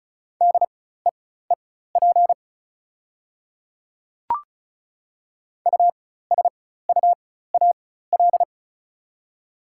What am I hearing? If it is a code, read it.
Morse code, heard as a keyed tone, about 36 wpm, spaced as for 16 wpm, 690 Hz. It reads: DEEP USUAL
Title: 4x Character Spacing - Top 500 Words - 35wpm